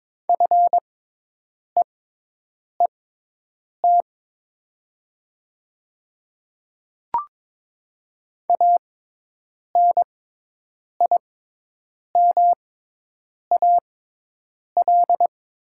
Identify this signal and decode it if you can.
Morse code, heard as a keyed tone, about 21 wpm, spaced as for 8 wpm, 700 Hz. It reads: FEET ANIMAL
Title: 6x Character Spacing - Top 500 Words - 22wpm